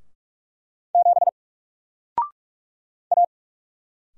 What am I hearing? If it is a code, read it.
Morse code, heard as a keyed tone, about 46 wpm, 700 Hz. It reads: 7 A
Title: Single Letter-Number 45wpm